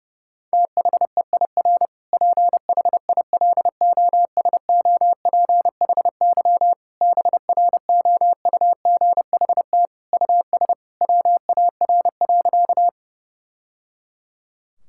Morse code, heard as a keyed tone, about 32 wpm, 700 Hz. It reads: THEIR PHILOSOPHY BROUGHT US WAR.
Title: Sentences from Top 1000 Words - Review Part 6 - 30wpm